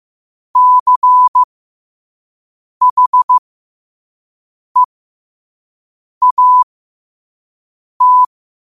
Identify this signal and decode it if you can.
Morse code, heard as a keyed tone, about 15 wpm, spaced as for 5 wpm, 1000 Hz. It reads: CHEAT
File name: 5wpm